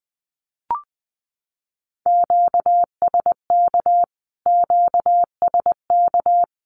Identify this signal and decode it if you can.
Morse code, heard as a keyed tone, about 20 wpm, 700 Hz. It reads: QSK QSK
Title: Q-Codes Repeated 3x 20wpm